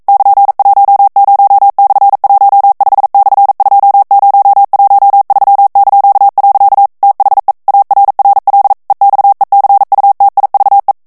Morse code, heard as a keyed tone, about 42 wpm, 800 Hz. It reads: <KN>10/15/2013). THE ARRL EXECUTIVE
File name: 40 mots minute